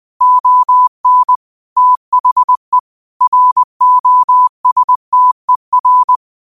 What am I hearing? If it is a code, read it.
Morse code, heard as a keyed tone, about 20 wpm, 1000 Hz. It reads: ON THE ROSTER